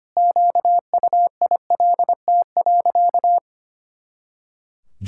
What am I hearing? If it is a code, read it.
Morse code, heard as a keyed tone, about 25 wpm, 700 Hz. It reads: QUILT.
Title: Pangram Challenge v2 - 25wpm